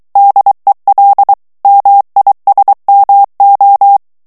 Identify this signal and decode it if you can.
Morse code, heard as a keyed tone, about 23 wpm, 800 Hz. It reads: DEL MISMO
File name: morse mp3 / ESPAGNOL TEXTE EN CLAIR 2 VITESSE 22 MOTS MINUTES